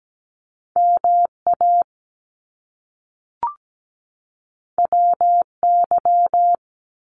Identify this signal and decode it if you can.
Morse code, heard as a keyed tone, about 17 wpm, 700 Hz. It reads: MA WY
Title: US States 17wpm